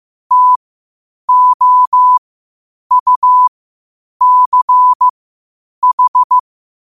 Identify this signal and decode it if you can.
Morse code, heard as a keyed tone, about 15 wpm, spaced as for 8 wpm, 1000 Hz. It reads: TOUCH
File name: qotdfiles / 8wpm